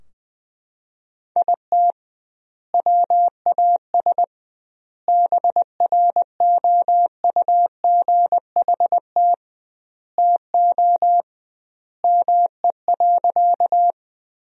Morse code, heard as a keyed tone, about 20 wpm, 700 Hz. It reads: IT WAS BROUGHT TO ME.
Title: 2x Word-Spacing - Sentences from Top 500 Words - 20wpm